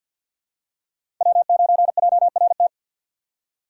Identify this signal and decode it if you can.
Morse code, heard as a keyed tone, about 48 wpm, 700 Hz. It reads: W9JRT
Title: USA Call Signs - 5 Characters - 4x Character Spacing - 50wpm